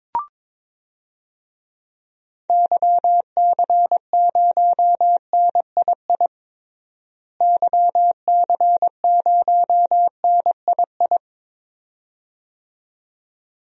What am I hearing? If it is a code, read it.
Morse code, heard as a keyed tone, about 22 wpm, 700 Hz. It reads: YC0NII YC0NII
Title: DXCC-100 Call Signs - 6 Characters - Repeated 2x - 22wpm